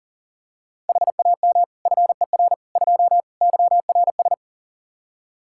Morse code, heard as a keyed tone, about 37 wpm, 690 Hz. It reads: HAM FER 2 YRS
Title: Practice QSO - Exchange 3 - More Info - 40wpm